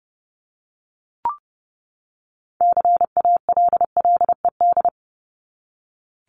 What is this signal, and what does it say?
Morse code, heard as a keyed tone, about 30 wpm, 700 Hz. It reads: CALLED
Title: ED Suffix - Top 100 Words with Base Words 30wpm